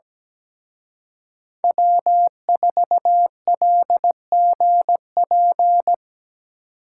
Morse code, heard as a keyed tone, about 20 wpm, 700 Hz. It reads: W4LGP
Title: Speed Racing - USA 1x3 Call Signs 17wpm